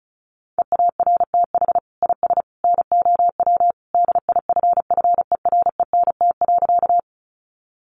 Morse code, heard as a keyed tone, about 34 wpm, 700 Hz. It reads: EARTH IS NOW DIFFERENT.
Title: Sentences from Top 600 Words 35wpm